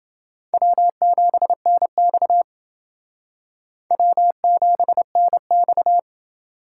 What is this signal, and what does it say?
Morse code, heard as a keyed tone, about 28 wpm, 690 Hz. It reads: W7NX W7NX